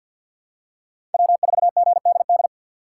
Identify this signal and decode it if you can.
Morse code, heard as a keyed tone, about 53 wpm, 700 Hz. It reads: W4GDD
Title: USA Call Signs - 5 Characters - 4x Character Spacing - 50wpm